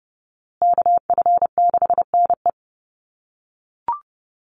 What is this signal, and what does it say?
Morse code, heard as a keyed tone, about 30 wpm, 700 Hz. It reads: KF6NE
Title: USA Call Signs Repeated 3x 30wpm